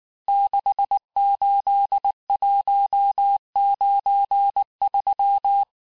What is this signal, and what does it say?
Morse code, heard as a keyed tone, about 19 wpm, 800 Hz. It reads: 68193
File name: cours-du-mercredi / audios / cours-869-texte-CHIFFRE-vitesse-18-MPM